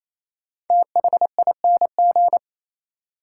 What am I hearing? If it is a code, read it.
Morse code, heard as a keyed tone, about 28 wpm, 700 Hz. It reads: THING
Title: Sets of 3 Words - New Word Drill - 28wpm